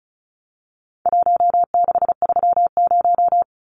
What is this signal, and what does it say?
Morse code, heard as a keyed tone, about 34 wpm, 700 Hz. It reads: 1630
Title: Numbers - Contest Style - 4-Digit 35wpm